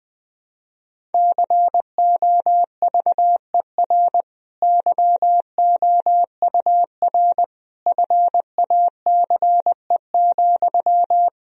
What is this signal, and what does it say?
Morse code, heard as a keyed tone, about 20 wpm, 700 Hz. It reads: COVER YOUR FACE,
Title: Sentences from Top 1000 Words - Review Part 1 - 20wpm